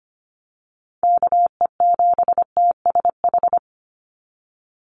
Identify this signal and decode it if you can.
Morse code, heard as a keyed tone, about 25 wpm, 700 Hz. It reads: KE7TSH
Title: USA Call Signs 25wpm